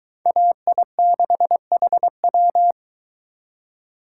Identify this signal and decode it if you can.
Morse code, heard as a keyed tone, about 23 wpm, 700 Hz. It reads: AI6HW